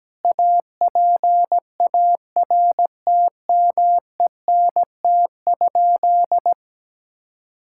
Morse code, heard as a keyed tone, about 17 wpm, 700 Hz. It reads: APARTMENT?
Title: Sentences from Top 1000 Words - Review Part 6 - 17wpm